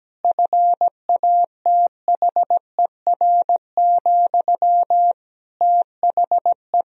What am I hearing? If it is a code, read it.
Morse code, heard as a keyed tone, about 17 wpm, 700 Hz. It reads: FATHER, THE
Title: Sentences from Top 1000 Words - Review Part 3 - 17wpm